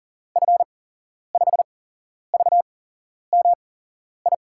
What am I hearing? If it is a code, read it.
Morse code, heard as a keyed tone, about 40 wpm, spaced as for 11 wpm, 700 Hz. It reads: F5VMI
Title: USA Call Signs - 6 Characters - 8x Character Spacing - 40wpm